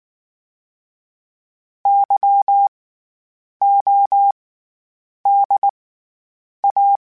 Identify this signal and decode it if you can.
Morse code, heard as a keyed tone, about 19 wpm, spaced as for 8 wpm, 800 Hz. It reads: YODA
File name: Special INGs / Special INGs-mp3s / SlowToFaster ing @18